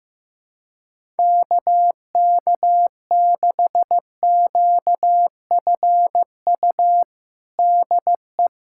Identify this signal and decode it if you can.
Morse code, heard as a keyed tone, about 15 wpm, 700 Hz. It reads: KK6QFU DE
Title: USA Callsign DE Callsign - 6 Characters - 15wpm